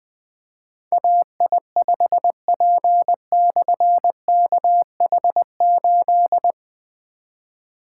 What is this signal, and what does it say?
Morse code, heard as a keyed tone, about 20 wpm, 700 Hz. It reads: AI5P/KH8
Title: Difficult Call Signs 20wpm